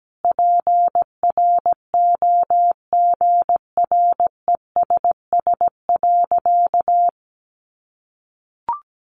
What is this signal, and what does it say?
Morse code, heard as a keyed tone, about 17 wpm, 700 Hz. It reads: PROGRESS.
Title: Sentences from Top 1400 Words 17wpm